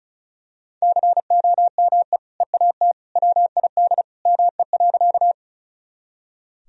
Morse code, heard as a keyed tone, about 36 wpm, 700 Hz. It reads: COME EAT WID ME.
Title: Sentences from Top 300 Words and Abbreviations 35wpm